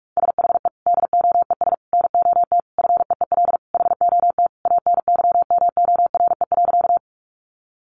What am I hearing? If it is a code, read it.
Morse code, heard as a keyed tone, about 44 wpm, 700 Hz. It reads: SHE DOES NOT FEEL HOT ANYMORE.
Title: Additional Easy Phrases 45wpm